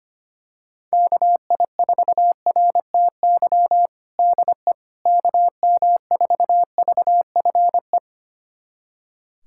Morse code, heard as a keyed tone, about 25 wpm, 700 Hz. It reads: KI4RTY DE KM4VFE
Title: USA Callsign DE Callsign - 6 Characters - 25wpm